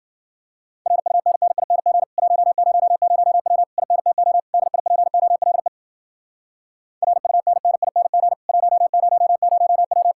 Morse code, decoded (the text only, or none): RUNNING 100W INTO DIPOLE RUNNING 100W